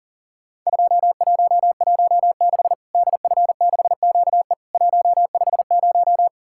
Morse code, heard as a keyed tone, about 40 wpm, 700 Hz. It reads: 2116 DF6QE 150
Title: Worked All Europe DX Contest - QTC Exchange - 40wpm